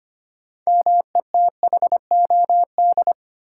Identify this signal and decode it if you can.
Morse code, heard as a keyed tone, about 25 wpm, 700 Hz. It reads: METHOD
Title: Sentences from Top 600 Words - New Word Drill - 25wpm